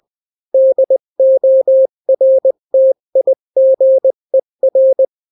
Morse code, heard as a keyed tone, about 20 wpm, 530 Hz. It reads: DORTIGER
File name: daily / 18wpm